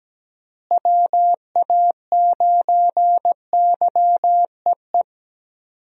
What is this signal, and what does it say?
Morse code, heard as a keyed tone, about 17 wpm, 700 Hz. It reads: WA9YEE
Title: USA Call Signs - 6 Characters - Repeated 3x - 17wpm